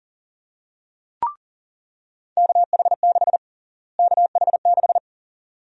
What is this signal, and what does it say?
Morse code, heard as a keyed tone, about 40 wpm, 700 Hz. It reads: KH6 KH6
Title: USA Call Sign Prefixes - Repeated 2x - 40wpm